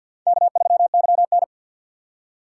Morse code, heard as a keyed tone, about 52 wpm, 700 Hz. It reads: K3YN
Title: USA Call Signs - 4 Characters - 3x Character Spacing - 50wpm